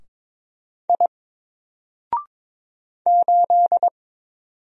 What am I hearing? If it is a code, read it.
Morse code, heard as a keyed tone, about 22 wpm, 700 Hz. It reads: I 8
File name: Single-Letter-Number-22wpm